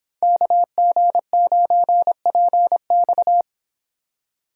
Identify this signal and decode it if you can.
Morse code, heard as a keyed tone, about 25 wpm, 700 Hz. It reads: KG9PX